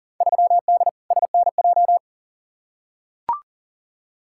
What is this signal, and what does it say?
Morse code, heard as a keyed tone, about 40 wpm, 700 Hz. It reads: 3D SNJ
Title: Field Day Exchanges Encore - 40wpm